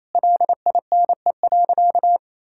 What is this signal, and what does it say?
Morse code, heard as a keyed tone, about 28 wpm, 700 Hz. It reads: LINE.
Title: Sentences from Top 1000 Words - Review Part 6 - 28wpm